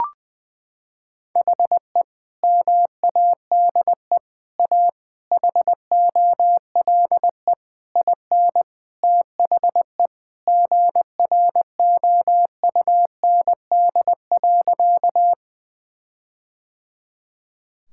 Morse code, heard as a keyed tone, about 20 wpm, 700 Hz. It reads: HE MADE A HOLE IN THE GROUND.
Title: Sentences from Top 1600 Words 20wpm